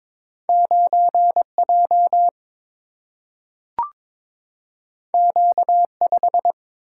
Morse code, heard as a keyed tone, about 22 wpm, 700 Hz. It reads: 9 J Q 5
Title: Single-Letter Single-Number 22wpm